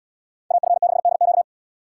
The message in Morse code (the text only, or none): AF6DQ